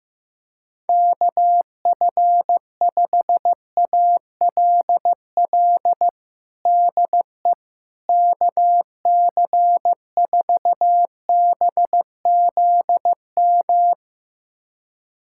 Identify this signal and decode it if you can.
Morse code, heard as a keyed tone, about 15 wpm, 700 Hz. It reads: KF5ALL DE KC4BZM
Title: USA Callsign DE Callsign - 6 Characters - 15wpm